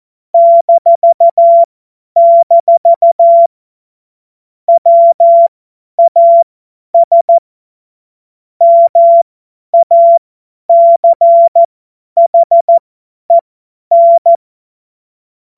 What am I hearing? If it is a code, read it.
Morse code, heard as a keyed tone, about 14 wpm, spaced as for 10 wpm, 680 Hz. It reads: -- WAS MACHEN
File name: Morsekurs / Witze / Witz 21. Juni 2022 50 F70